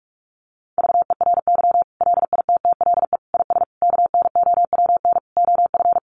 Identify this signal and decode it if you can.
Morse code, heard as a keyed tone, about 44 wpm, 700 Hz. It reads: VERY LITTLE IS KNOWN OF
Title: Sentences from Top 1500 Words 45wpm